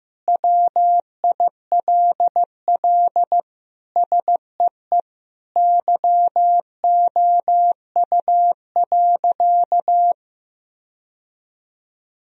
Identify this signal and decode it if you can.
Morse code, heard as a keyed tone, about 16 wpm, 700 Hz. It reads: WILL SEE YOU.
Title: Sentences from Top 1000 Words - Review Part 1 - 15wpm